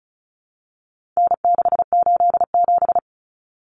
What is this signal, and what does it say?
Morse code, heard as a keyed tone, about 34 wpm, 700 Hz. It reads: N687
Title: Common Cut Numbers - 4-Digit 35wpm